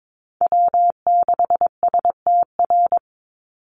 Morse code, heard as a keyed tone, about 22 wpm, 700 Hz. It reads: W6STR
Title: Call Signs Repeated 3x - 22wpm